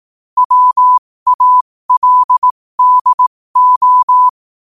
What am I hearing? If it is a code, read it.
Morse code, heard as a keyed tone, about 18 wpm, spaced as for 15 wpm, 1000 Hz. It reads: WALDO
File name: qotdfiles / 15wpm